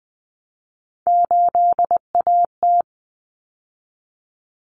Morse code, heard as a keyed tone, about 20 wpm, 700 Hz. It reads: 8AT
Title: Common Cut Numbers - 3-Digit 20wpm